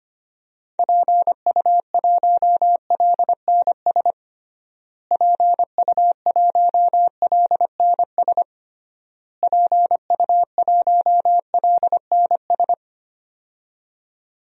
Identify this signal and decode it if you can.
Morse code, heard as a keyed tone, about 25 wpm, 700 Hz. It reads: PU1LNS PU1LNS PU1LNS
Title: DXCC-100 Call Signs - 6 Characters - Repeated 3x - 25wpm